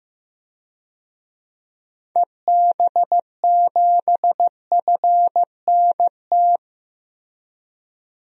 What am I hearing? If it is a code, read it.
Morse code, heard as a keyed tone, about 16 wpm, 700 Hz. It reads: EB7FNT
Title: DXCC-100 Call Signs - 6 Characters - Repeated 2x - 15wpm